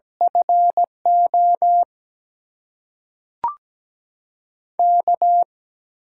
Morse code, heard as a keyed tone, about 17 wpm, 700 Hz. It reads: FO K
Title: USA Call Signs - 6 Characters - 8x Character Spacing - 17wpm